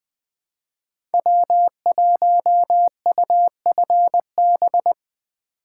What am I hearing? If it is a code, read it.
Morse code, heard as a keyed tone, about 21 wpm, 700 Hz. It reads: W1UFB
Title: USA Call Signs - 5 Characters - 2x Character Spacing - 20wpm